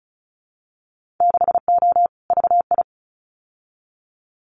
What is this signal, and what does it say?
Morse code, heard as a keyed tone, about 36 wpm, 700 Hz. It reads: 6O VI
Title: Winter Field Day Exchanges Encore 35wpm